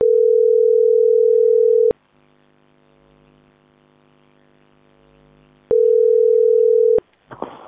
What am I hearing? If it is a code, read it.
Ringback tone (North America), 440 + 480 Hz.